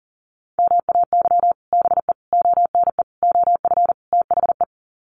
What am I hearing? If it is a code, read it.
Morse code, heard as a keyed tone, about 40 wpm, 700 Hz. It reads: MAY BE ONE OF THE
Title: Sets of 5 Words Encore2 40wpm